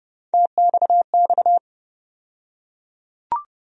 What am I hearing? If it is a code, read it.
Morse code, heard as a keyed tone, about 30 wpm, 700 Hz. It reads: TXX
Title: USA Call Sign Postfixes - 3 Letters 30wpm